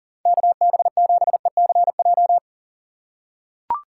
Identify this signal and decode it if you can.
Morse code, heard as a keyed tone, about 37 wpm, 700 Hz. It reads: KB7ECJ
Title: USA Call Signs - 6 Characters - 4x Character Spacing - 40wpm